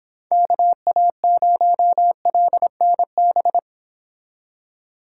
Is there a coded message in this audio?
Yes: Morse code from 0.3 to 3.6 s, heard as a keyed tone, about 26 wpm, 700 Hz. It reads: KA0LNB